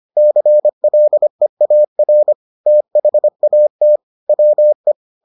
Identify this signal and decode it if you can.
Morse code, heard as a keyed tone, about 25 wpm, 600 Hz. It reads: CLEAR THAT WE